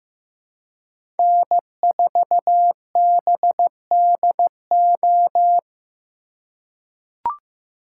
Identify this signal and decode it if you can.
Morse code, heard as a keyed tone, about 15 wpm, 700 Hz. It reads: N4BDO
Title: USA Call Signs - 4x Character Spacing - 15wpm